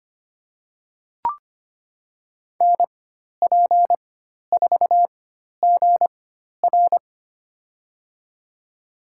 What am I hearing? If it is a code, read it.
Morse code, heard as a keyed tone, about 25 wpm, spaced as for 12 wpm, 700 Hz. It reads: NP4GR
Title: USA Call Signs - 5 Characters - 4x Character Spacing - 25wpm